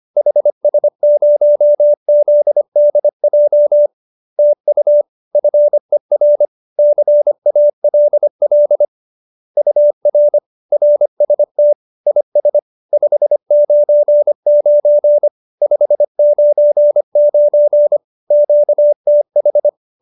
Morse code, heard as a keyed tone, about 25 wpm, 590 Hz. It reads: HS0ZDJ TU FER CALL UR RST IS 599 599 QTH